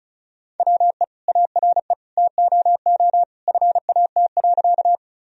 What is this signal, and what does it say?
Morse code, heard as a keyed tone, about 34 wpm, 700 Hz. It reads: WE ARE TOO FAT.
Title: Sentences from Top 1000 Words - Review Part 4 - 35wpm